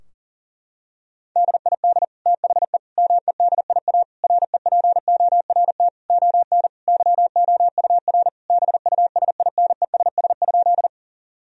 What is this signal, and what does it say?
Morse code, heard as a keyed tone, about 38 wpm, 700 Hz. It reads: DID THE MEDIA REPORT ON YOUR BUSINESS?
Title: Sentences from Top 1000 Words - Review Part 4 - 40wpm